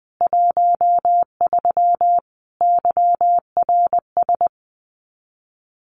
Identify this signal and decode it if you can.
Morse code, heard as a keyed tone, about 20 wpm, 700 Hz. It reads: 13 YRS
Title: Common QSO Phrases 20wpm